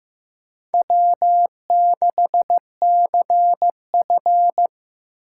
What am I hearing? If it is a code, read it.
Morse code, heard as a keyed tone, about 15 wpm, 700 Hz. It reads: W6CF
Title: USA Call Signs - 4 Characters - 4x Character Spacing - 15wpm